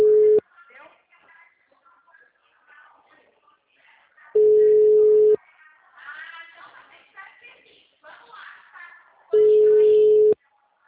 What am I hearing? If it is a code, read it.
Ringback tone (Europe, 425 Hz), with speech.
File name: out- -2017- - - .741